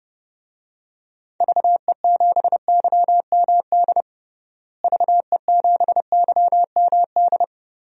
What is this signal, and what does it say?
Morse code, heard as a keyed tone, about 30 wpm, 700 Hz. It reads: VE7YMD VE7YMD
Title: DXCC-100 Call Signs - 6 Characters - Repeated 2x - 30wpm